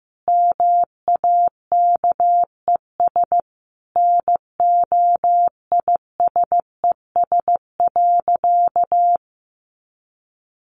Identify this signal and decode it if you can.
Morse code, heard as a keyed tone, about 14 wpm, 700 Hz. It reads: MAKES NOISES.
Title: Easy Phrases 15wpm